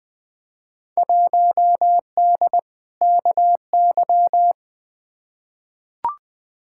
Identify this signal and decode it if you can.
Morse code, heard as a keyed tone, about 20 wpm, 700 Hz. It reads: 1D KY
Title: Field Day Exchanges - 20wpm